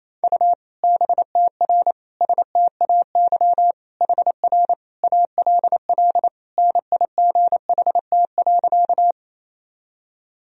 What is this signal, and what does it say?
Morse code, heard as a keyed tone, about 28 wpm, 700 Hz. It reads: U BTR STAY HR ALL NIGHT.
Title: Sentences from Top 500 Words and Abbreviations 28wpm